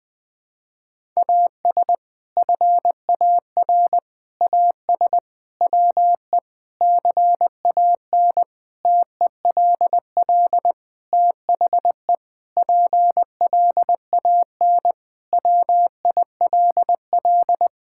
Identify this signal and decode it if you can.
Morse code, heard as a keyed tone, about 20 wpm, 700 Hz. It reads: AS FAR AS WE CAN TELL THE PLAN WILL
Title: Sentences with 6-Word Phrases - Spoken Beginning - 20wpm